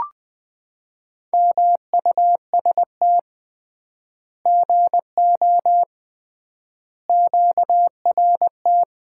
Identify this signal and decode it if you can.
Morse code, heard as a keyed tone, about 20 wpm, 700 Hz. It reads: MUST GO QRT
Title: Common QSO Phrases - 3x Word-Spacing - 20wpm